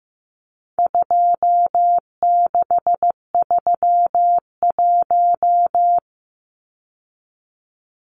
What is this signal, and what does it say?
Morse code, heard as a keyed tone, about 15 wpm, 700 Hz. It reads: 2631
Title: Numbers - 4-Digit 15wpm